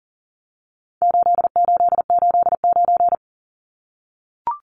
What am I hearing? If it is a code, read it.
Morse code, heard as a keyed tone, about 40 wpm, 700 Hz. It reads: 8889
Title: Numbers - 4-Digit 40wpm